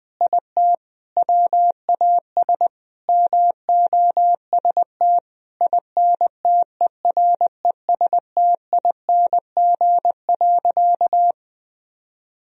Morse code, heard as a keyed tone, about 20 wpm, 700 Hz. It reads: IT WAS MOST INTERESTING.
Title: Sentences from Top 1000 Words - Review Part 1 - 20wpm